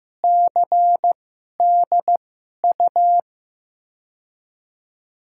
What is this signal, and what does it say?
Morse code, heard as a keyed tone, about 15 wpm, spaced as for 11 wpm, 700 Hz. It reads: CDU
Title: USA Call Signs - 6 Characters - 2x Character Spacing - 15wpm